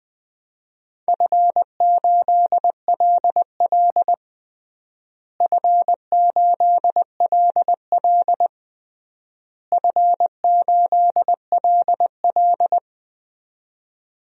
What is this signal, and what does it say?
Morse code, heard as a keyed tone, about 20 wpm, 690 Hz. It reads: F8LL F8LL F8LL
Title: DXCC-100 Call Signs - 4 Characters - Repeated 3x - 20wpm